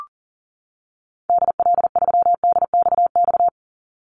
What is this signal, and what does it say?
Morse code, heard as a keyed tone, about 40 wpm, 700 Hz. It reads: DL3DXX
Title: Call Signs - 40wpm